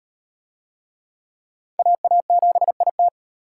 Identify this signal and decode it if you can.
Morse code, heard as a keyed tone, about 32 wpm, 700 Hz. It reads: AA7IT